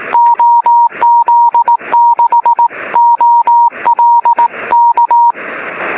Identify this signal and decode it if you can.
Morse code, heard as a keyed tone, about 18 wpm, 950 Hz. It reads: OZ6OL K
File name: sm7lcb / 16-oz6ol 23cm